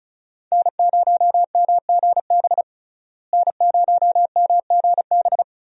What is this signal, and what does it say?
Morse code, heard as a keyed tone, about 36 wpm, 700 Hz. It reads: N0MGB N0MGB
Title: USA Call Signs - 5 Characters - Repeated 3x - 35wpm